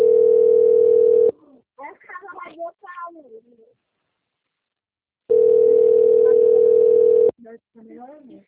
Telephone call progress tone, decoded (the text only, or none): ringback tone